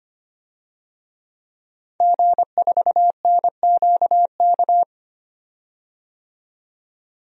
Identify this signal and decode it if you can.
Morse code, heard as a keyed tone, about 25 wpm, 700 Hz. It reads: G4NQK
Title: DXCC-100 Call Signs - 5 Characters - 25wpm